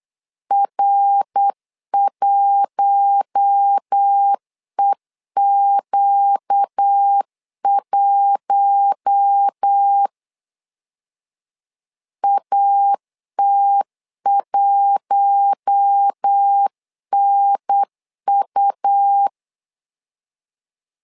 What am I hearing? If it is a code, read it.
Morse code, heard as a keyed tone, about 8 wpm, 800 Hz. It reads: R1EQ1 AT1NU
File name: LEZIONI / 4 Q1 8wpm 3035